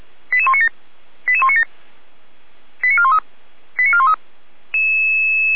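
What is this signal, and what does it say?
Five-tone selective call (ZVEI), heard as quick runs of five tones, tones of about 70 ms. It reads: 88197 88197 89413 89413
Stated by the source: ZVEI five-tone selective call received with an SDR.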